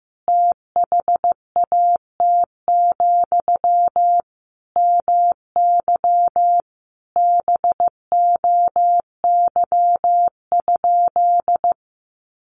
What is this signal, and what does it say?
Morse code, heard as a keyed tone, about 15 wpm, 700 Hz. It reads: THAT, MY BOY?
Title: Sentences from Top 1100 Words 15wpm